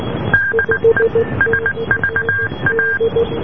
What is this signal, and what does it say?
Morse code, heard as a keyed tone, about 19 wpm, 1600 Hz, with noise. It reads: DE SVA